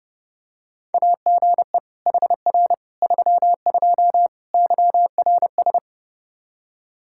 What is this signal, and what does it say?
Morse code, heard as a keyed tone, about 32 wpm, 700 Hz. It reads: AGE HR 32 YRS
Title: Practice QSO - Exchange 3 - More Info - 30wpm